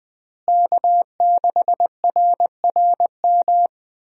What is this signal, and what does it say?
Morse code, heard as a keyed tone, about 20 wpm, 700 Hz. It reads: K6RRM
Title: USA Call Signs - 5 Characters - Repeated 2x - 20wpm